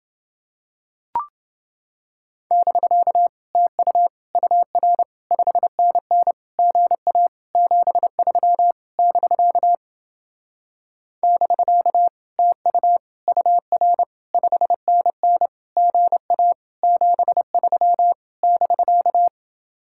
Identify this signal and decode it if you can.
Morse code, heard as a keyed tone, about 31 wpm, 700 Hz. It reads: <BK> TU UR 5NN GA 73 <BK> <BK> TU UR 5NN GA 73 <BK>